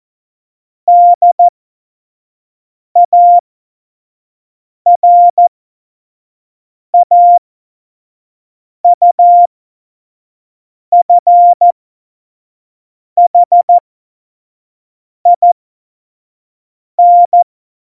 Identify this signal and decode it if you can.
Morse code, heard as a keyed tone, about 14 wpm, spaced as for 5 wpm, 700 Hz. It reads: DARAUFHIN
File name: Morsekurs / Witze / Witz 21 Juli 2020, 25 F70